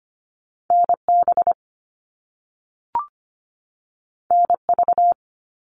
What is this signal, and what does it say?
Morse code, heard as a keyed tone, about 25 wpm, 700 Hz. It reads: NB NV
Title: All Cut Numbers - 2-Digit 25wpm